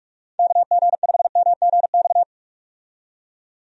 Morse code, heard as a keyed tone, about 46 wpm, 700 Hz. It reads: KG5MGX
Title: USA Call Signs - 6 Characters - 45wpm